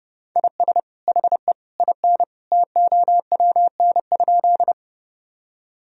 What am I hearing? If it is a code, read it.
Morse code, heard as a keyed tone, about 30 wpm, 700 Hz. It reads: IS HE IN TOWN?
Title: Sentences from Top 1000 Words - Review Part 2 - 30wpm